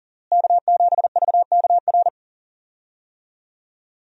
Morse code, heard as a keyed tone, about 40 wpm, 700 Hz. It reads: K7VKR
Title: USA Call Signs - 5 Characters - Repeated 2x - 40wpm